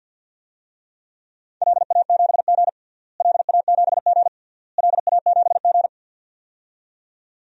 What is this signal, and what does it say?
Morse code, heard as a keyed tone, about 51 wpm, 700 Hz. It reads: LA7G LA7G LA7G
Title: DXCC-100 Call Signs - 4 Characters - Repeated 3x - 50wpm